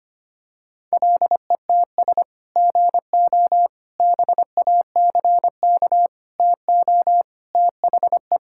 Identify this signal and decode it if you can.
Morse code, heard as a keyed tone, about 25 wpm, 700 Hz. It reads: LETS GO BACK TO THE
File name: practice / 25wpm / Sentences with 3-Word Phrases - 25wpm